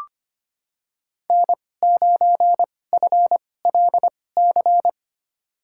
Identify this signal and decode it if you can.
Morse code, heard as a keyed tone, about 25 wpm, spaced as for 18 wpm, 700 Hz. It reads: N9FLC
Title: USA Call Signs - 5 Characters - 2x Character Spacing - 25wpm